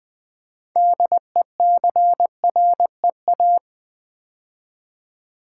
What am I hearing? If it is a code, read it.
Morse code, heard as a keyed tone, about 20 wpm, 700 Hz. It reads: DECREA
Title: Build A Word - 2 Letters At A Time - 8-Letter Words - 20wpm